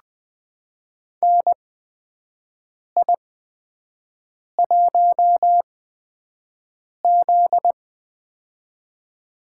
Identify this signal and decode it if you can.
Morse code, heard as a keyed tone, about 21 wpm, spaced as for 5 wpm, 700 Hz. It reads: NI1Z
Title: USA Call Signs - 4 Characters - 8x Character Spacing - 20wpm